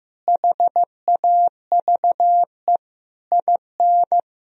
Morse code, heard as a keyed tone, about 15 wpm, 700 Hz. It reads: HAVE IN
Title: Sentences from Top 1000 Words - Review Part 3 - 15wpm